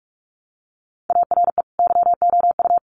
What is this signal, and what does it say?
Morse code, heard as a keyed tone, about 44 wpm, 700 Hz. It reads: ARE YOU
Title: Sentences from Top 200 Words 45wpm